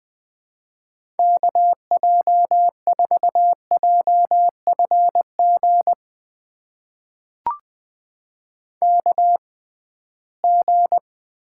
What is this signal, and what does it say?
Morse code, heard as a keyed tone, about 20 wpm, spaced as for 15 wpm, 700 Hz. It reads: KJ4JFG KG
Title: USA Call Signs - 6x Character Spacing - 20wpm